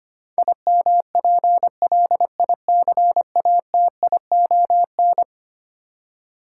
Morse code, heard as a keyed tone, about 25 wpm, 700 Hz. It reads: IMPLICATION
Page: morse-code-ninja-practice-25wpm